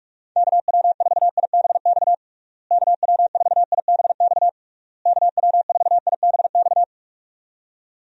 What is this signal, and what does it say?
Morse code, heard as a keyed tone, about 46 wpm, 700 Hz. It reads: KW4IBX KW4IBX KW4IBX
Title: USA Call Signs - 6 Characters - Repeated 3x - 45wpm